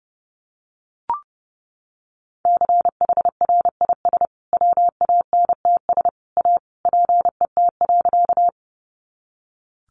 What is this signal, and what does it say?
Morse code, heard as a keyed tone, about 30 wpm, 700 Hz. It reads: CHRIS WANTS A PET.
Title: Additional Easy Phrases 30wpm